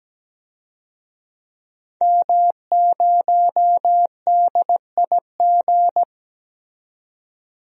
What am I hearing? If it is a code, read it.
Morse code, heard as a keyed tone, about 17 wpm, 700 Hz. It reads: M0DIG
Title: DXCC-100 Call Signs - 5 Characters - 17wpm